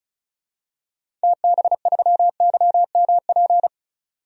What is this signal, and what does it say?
Morse code, heard as a keyed tone, about 36 wpm, 700 Hz. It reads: TB3YMP
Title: DXCC-100 Call Signs - 6 Characters - 35wpm